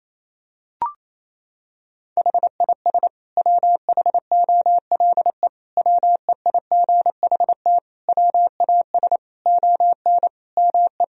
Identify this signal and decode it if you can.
Morse code, heard as a keyed tone, about 28 wpm, 700 Hz. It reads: HIS WHOLE WEIGHT WAS ON ME
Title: Sentences from Top 1000 Words - Review Part 3 - 28wpm